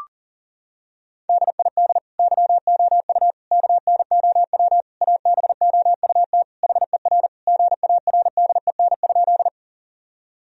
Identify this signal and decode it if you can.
Morse code, heard as a keyed tone, about 40 wpm, 700 Hz. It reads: DID YOU KNOW ABOUT HER GARDEN?
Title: Sentences from Top 1000 Words - Review Part 4 - 40wpm